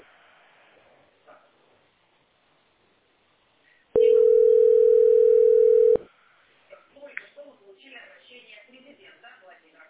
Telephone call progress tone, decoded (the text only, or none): ringback tone